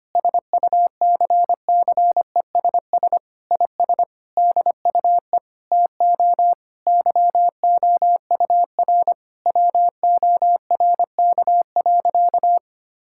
Morse code, heard as a keyed tone, about 25 wpm, 700 Hz. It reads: SUCCESS IS DUE TO YOUR WORK.
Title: Sentences from Top 1000 Words - Review Part 6 - 25wpm